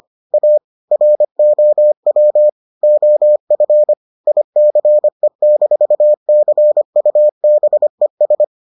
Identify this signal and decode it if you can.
Morse code, heard as a keyed tone, about 25 wpm, 600 Hz. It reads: A ROW OF ICE-CUBES